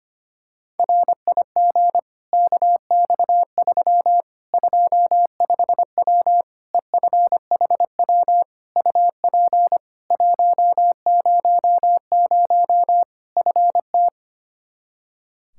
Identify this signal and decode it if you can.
Morse code, heard as a keyed tone, about 25 wpm, 700 Hz. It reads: RIG KX3 25W EFHW UP 100 FT
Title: Practice QSO - Exchange 2 - Info - 25wpm